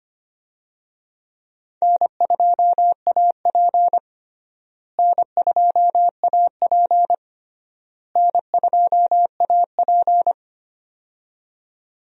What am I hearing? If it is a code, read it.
Morse code, heard as a keyed tone, about 25 wpm, 700 Hz. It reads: N2AP N2AP N2AP
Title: USA Call Signs - 4 Characters - Repeated 3x - 25wpm